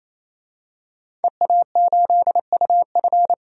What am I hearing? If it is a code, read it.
Morse code, heard as a keyed tone, about 28 wpm, 700 Hz. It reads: EA8UF
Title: DXCC-100 Call Signs - 5 Characters - 28wpm